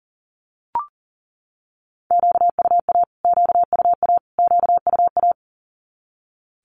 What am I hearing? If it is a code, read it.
Morse code, heard as a keyed tone, about 40 wpm, 700 Hz. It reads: QUA QUA QUA
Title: Q-Codes Repeated 3x 40wpm